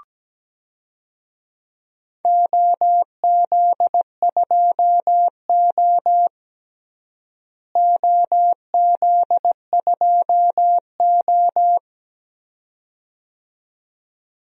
Morse code, heard as a keyed tone, about 17 wpm, 700 Hz. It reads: OZ2O OZ2O
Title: DXCC-100 Call Signs - 4 Characters - Repeated 2x - 17wpm